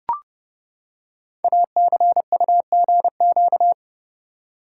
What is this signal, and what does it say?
Morse code, heard as a keyed tone, about 30 wpm, 700 Hz. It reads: ACUGQ
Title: Code Groups - 5 Letters - 30wpm